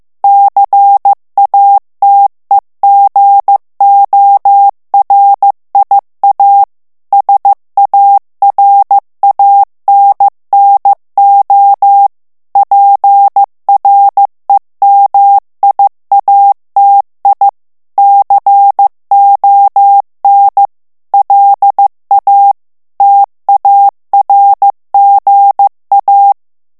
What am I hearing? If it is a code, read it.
Morse code, heard as a keyed tone, about 15 wpm, 800 Hz. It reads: CATEGORIA SARANNO PREMIATI CON LA TARGA
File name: morse mp3 / ITALIEN texte clair 2 vitesse 14 mots minute